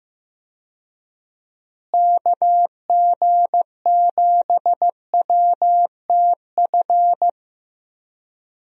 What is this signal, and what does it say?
Morse code, heard as a keyed tone, about 15 wpm, 700 Hz. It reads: KG7WTF
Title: USA Call Signs - 6 Characters - Repeated 2x - 15wpm